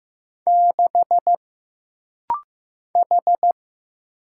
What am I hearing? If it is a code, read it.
Morse code, heard as a keyed tone, about 15 wpm, 700 Hz. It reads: 6 H
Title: Single Letter-Number Alt - Rapid-Fire 15wpm